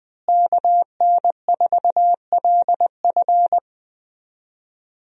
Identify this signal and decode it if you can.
Morse code, heard as a keyed tone, about 20 wpm, 700 Hz. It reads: KN4LF
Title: USA Call Signs - 5 Characters - Repeated 2x - 20wpm